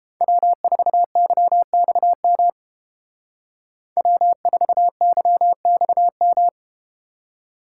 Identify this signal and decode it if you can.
Morse code, heard as a keyed tone, about 30 wpm, 700 Hz. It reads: W4YXM W4YXM